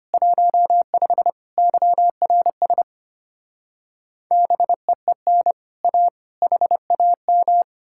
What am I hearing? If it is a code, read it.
Morse code, heard as a keyed tone, about 33 wpm, 700 Hz. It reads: 15 YRS BEEN A HAM